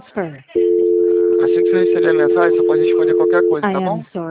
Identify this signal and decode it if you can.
Dial tone (North America), 350 + 440 Hz, with speech.
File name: out- - - - - .77404